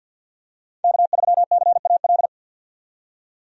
Morse code, heard as a keyed tone, about 51 wpm, 700 Hz. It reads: K3CAL
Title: USA Call Signs - 5 Characters - 50wpm